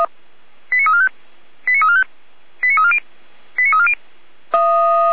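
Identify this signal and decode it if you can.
Five-tone selective call (ZVEI), heard as quick runs of five tones, tones of about 70 ms. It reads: 89345 89345 89340 89340
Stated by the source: Five-tone ZVEI selective call received with an SDR.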